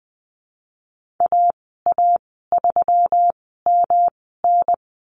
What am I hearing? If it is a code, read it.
Morse code, heard as a keyed tone, about 20 wpm, spaced as for 14 wpm, 700 Hz. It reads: AA3MN
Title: 245 Lesson 39g - Letter z - Call Signs 20wpm